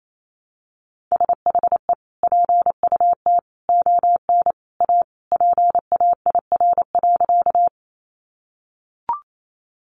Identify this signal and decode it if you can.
Morse code, heard as a keyed tone, about 28 wpm, 700 Hz. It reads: SHE PUT ON A PAIR.
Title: Sentences from Top 1200 Words 28wpm